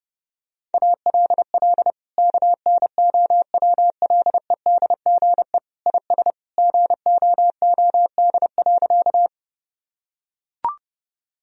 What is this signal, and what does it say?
Morse code, heard as a keyed tone, about 32 wpm, 700 Hz. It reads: ALL KNOWLEDGE IS GOOD.
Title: Sentences from Top 1000 Words - Review Part 3 - 30wpm